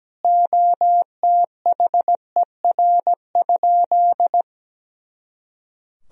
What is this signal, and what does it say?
Morse code, heard as a keyed tone, about 17 wpm, 700 Hz. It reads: OTHER?
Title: Sentences from Top 200 Words and Abbreviations 17wpm